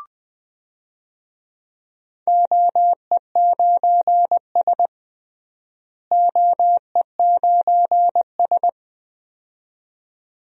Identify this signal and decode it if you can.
Morse code, heard as a keyed tone, about 20 wpm, 700 Hz. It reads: OE9S OE9S
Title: DXCC-100 Call Signs - 4 Characters - Repeated 2x - 20wpm